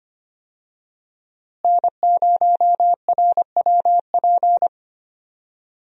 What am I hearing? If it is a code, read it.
Morse code, heard as a keyed tone, about 25 wpm, 700 Hz. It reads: N0RWP
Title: USA Call Signs - 5 Characters - Repeated 2x - 25wpm